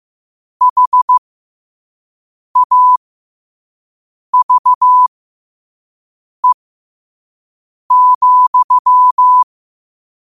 Morse code, heard as a keyed tone, about 15 wpm, spaced as for 5 wpm, 1000 Hz. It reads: HAVE,
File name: qotdfiles / 5wpm